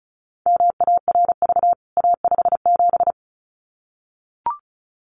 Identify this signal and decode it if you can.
Morse code, heard as a keyed tone, about 34 wpm, 700 Hz. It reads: MARV A57
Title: CWT Exchange 35wpm